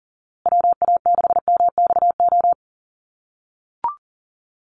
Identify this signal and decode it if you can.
Morse code, heard as a keyed tone, about 40 wpm, 700 Hz. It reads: WA6MXO
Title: USA Call Signs Repeated 2x 40wpm